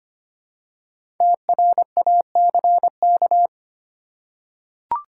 Morse code, heard as a keyed tone, about 25 wpm, 700 Hz. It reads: TRACK
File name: 5 Letter Words - First 3 Letters Given - 25wpm